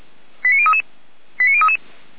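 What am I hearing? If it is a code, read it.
Five-tone selective call (ZVEI), heard as quick runs of five tones, tones of about 80 ms. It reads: 89033 89033
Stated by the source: ZVEI five-tone selective call 89187 received by SDR.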